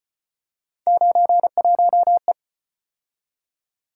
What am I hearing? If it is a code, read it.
Morse code, heard as a keyed tone, about 34 wpm, 700 Hz. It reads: 91E